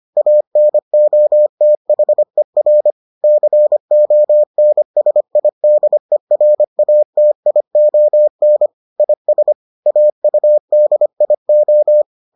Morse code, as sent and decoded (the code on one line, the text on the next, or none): .- -. --- - .... .-. / -.-. --- -. ... .. -.. .-. .- - .. --- -. / .. ... / .- ..- -.. .. ---
ANOTHER CONSIDERATION IS AUDIO